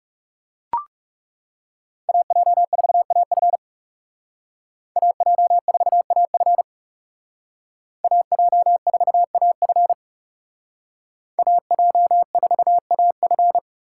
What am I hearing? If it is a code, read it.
Morse code, heard as a keyed tone, about 36 wpm, 700 Hz. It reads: AJ4AF AJ4AF AJ4AF AJ4AF
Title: Speed Racing - USA 2x2 Call Signs 30wpm